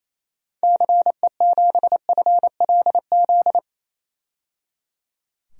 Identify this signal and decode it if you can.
Morse code, heard as a keyed tone, about 28 wpm, 700 Hz. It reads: CE7FLZ
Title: DXCC-100 Call Signs - 6 Characters - 28wpm